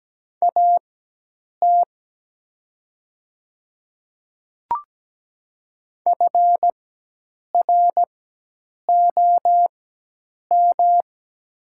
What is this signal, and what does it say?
Morse code, heard as a keyed tone, about 17 wpm, spaced as for 8 wpm, 700 Hz. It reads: AT FROM
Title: 4x Character Spacing - Top 200 Words - 17wpm